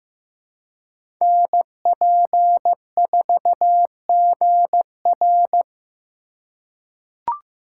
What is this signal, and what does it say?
Morse code, heard as a keyed tone, about 15 wpm, 700 Hz. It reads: NP4GR
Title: USA Call Signs - 3x Character Spacing - 15wpm